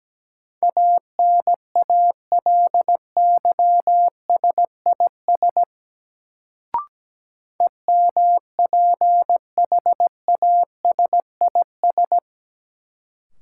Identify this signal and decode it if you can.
Morse code, heard as a keyed tone, about 17 wpm, 700 Hz. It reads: ANALYSIS EMPHASIS
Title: Sentences from Top 1400 Words - New Word Drill Alt - 17wpm